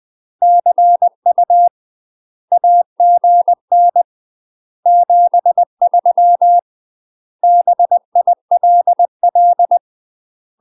Morse code, heard as a keyed tone, about 20 wpm, 700 Hz. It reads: CU AGN 73 BILL
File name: UK MASTER QSOs TRAINING/20WPM